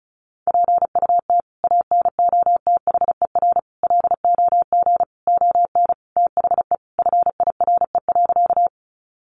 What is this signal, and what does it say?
Morse code, heard as a keyed tone, about 34 wpm, 700 Hz. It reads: PUT ANOTHER LOG ON THE FIRE.
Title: Sentences from Top 1400 Words 35wpm